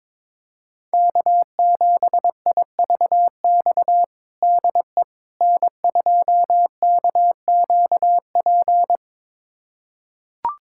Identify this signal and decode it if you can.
Morse code, heard as a keyed tone, about 22 wpm, 700 Hz. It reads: K7IVX DE N2KQP
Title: USA Callsign DE Callsign - 5 Characters - 22wpm